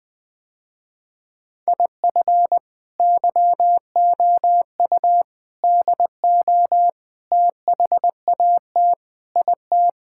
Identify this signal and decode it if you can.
Morse code, heard as a keyed tone, about 20 wpm, 700 Hz. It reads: IF YOU DO THAT IT
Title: Sentences with 3-Word Phrases - Spoken Beginning - 20wpm